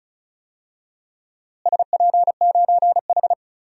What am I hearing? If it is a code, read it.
Morse code, heard as a keyed tone, about 36 wpm, 700 Hz. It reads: SP9H
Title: DXCC-100 Call Signs - 4 Characters - Repeated 3x - 35wpm